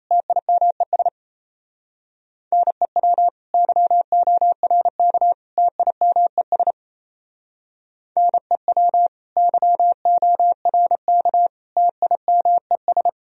Morse code, heard as a keyed tone, about 31 wpm, 700 Hz. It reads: TIMES NEW YORK TIMES NEW YORK TIMES
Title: Speed-Racing - Sets of 3 Words 28wpm